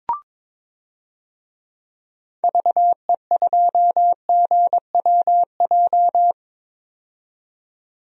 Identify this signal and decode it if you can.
Morse code, heard as a keyed tone, about 22 wpm, 700 Hz. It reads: VE2GWJ
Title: DXCC-100 Call Signs - 6 Characters - 22wpm